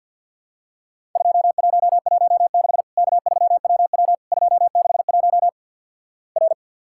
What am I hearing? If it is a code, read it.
Morse code, heard as a keyed tone, about 50 wpm, 700 Hz. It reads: 2116 K3WW 261 R
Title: Worked All Europe DX Contest - QTC Exchange with Repeats - 50wpm